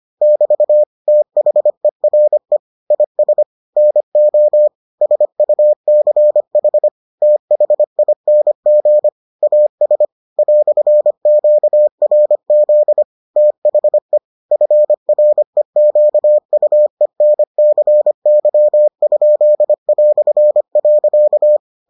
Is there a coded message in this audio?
Yes: Morse code from 0.2 to 21.6 s, heard as a keyed tone, about 25 wpm, 600 Hz. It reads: <BT> THERE IS NO SUCH THING AS "QRZ THE FREQUENCY?".